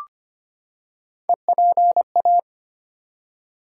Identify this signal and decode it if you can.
Morse code, heard as a keyed tone, about 25 wpm, 700 Hz. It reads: EPA
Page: morse-code-ninja-practice-25wpm